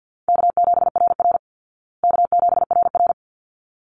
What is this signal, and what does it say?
Morse code, heard as a keyed tone, about 50 wpm, 700 Hz. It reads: K7RR K7RR
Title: Easy Call Signs Repeated 2x - 50wpm